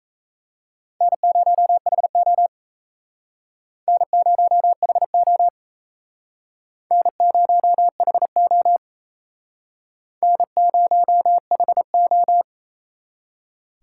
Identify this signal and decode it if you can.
Morse code, heard as a keyed tone, about 36 wpm, 700 Hz. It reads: N0HO N0HO N0HO N0HO